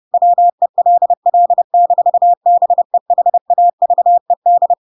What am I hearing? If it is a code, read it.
Morse code, heard as a keyed tone, about 30 wpm, 700 Hz. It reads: WELL-BEHAVED
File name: Pinocchio / Pinocchio ch3 30 30 ik1hge